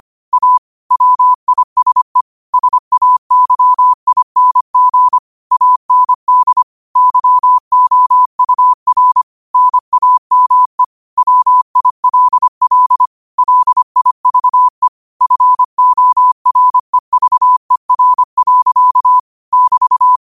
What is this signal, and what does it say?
Morse code, heard as a keyed tone, about 25 wpm, 1000 Hz. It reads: A WISE SAYING AND YOUR NAME WILL LIVE FOREVER. <BT>